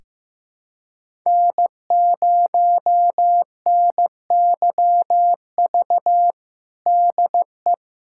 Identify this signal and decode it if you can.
Morse code, heard as a keyed tone, about 16 wpm, 700 Hz. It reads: N0NYV DE
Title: USA Callsign DE Callsign - 5 Characters - 15wpm